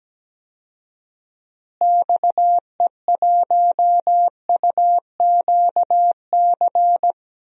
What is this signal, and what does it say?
Morse code, heard as a keyed tone, about 17 wpm, 700 Hz. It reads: XE1UQC
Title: DXCC-100 Call Signs - 6 Characters - Repeated 3x - 17wpm